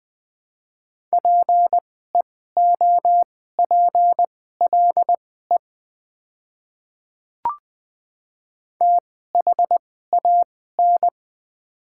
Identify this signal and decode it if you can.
Morse code, heard as a keyed tone, about 21 wpm, spaced as for 15 wpm, 690 Hz. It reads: PEOPLE THAN
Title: 2x Character Spacing - Top 100 Words - 20wpm